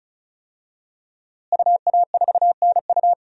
Morse code, heard as a keyed tone, about 34 wpm, 700 Hz. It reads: UA4NU
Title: DXCC-100 Call Signs - 5 Characters - Repeated 3x - 35wpm